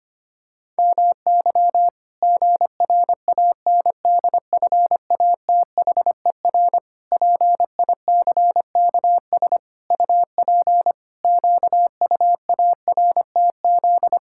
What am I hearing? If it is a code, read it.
Morse code, heard as a keyed tone, about 25 wpm, 700 Hz. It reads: MY GRANDFATHER PICKS UP QUARTZ
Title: Pangram Challenge v2 - 25wpm